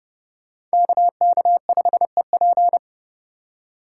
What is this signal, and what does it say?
Morse code, heard as a keyed tone, about 29 wpm, 700 Hz. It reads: KK5EP